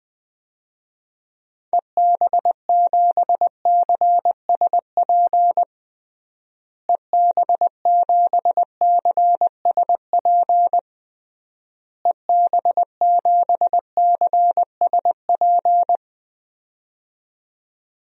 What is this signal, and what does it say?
Morse code, heard as a keyed tone, about 21 wpm, 700 Hz. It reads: EB7CSP EB7CSP EB7CSP
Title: DXCC-100 Call Signs - 6 Characters - Repeated 3x - 20wpm